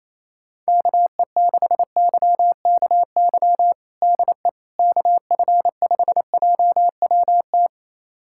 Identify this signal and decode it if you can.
Morse code, heard as a keyed tone, about 28 wpm, 700 Hz. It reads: KE6YKY DE KF5JWT
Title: USA Callsign DE Callsign - 6 Characters - 28wpm